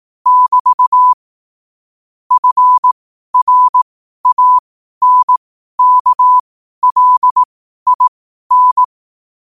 Morse code, heard as a keyed tone, about 18 wpm, spaced as for 12 wpm, 1000 Hz. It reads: <BT> FRANKLIN
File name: qotdfiles / 12wpm